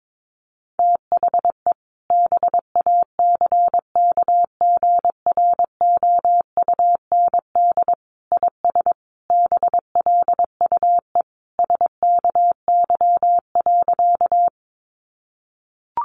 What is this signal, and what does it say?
Morse code, heard as a keyed tone, about 22 wpm, 700 Hz. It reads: THE BACKGROUND IS BLUE SKY.
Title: Sentences from Top 1300 Words 22wpm